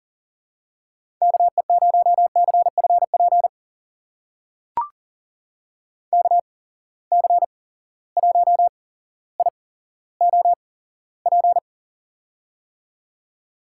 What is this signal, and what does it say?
Morse code, heard as a keyed tone, about 40 wpm, spaced as for 17 wpm, 700 Hz. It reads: KE0CFP KC1IOP
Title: USA Call Signs - 6 Characters - 8x Character Spacing - 40wpm